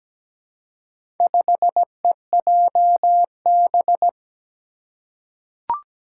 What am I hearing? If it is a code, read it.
Morse code, heard as a keyed tone, about 17 wpm, 700 Hz. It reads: 5EJB
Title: Four Challenging Letter-Number Combinations 17wpm